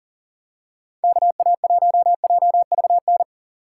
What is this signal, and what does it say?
Morse code, heard as a keyed tone, about 40 wpm, 700 Hz. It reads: KA1JVN
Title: USA Call Signs - 3x Character Spacing - 40wpm